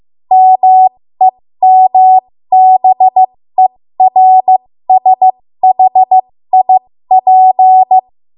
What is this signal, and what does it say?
Morse code, heard as a keyed tone, about 15 wpm, spaced as for 13 wpm, 750 Hz. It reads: MEMBERSHIP
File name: Morse / 13WPM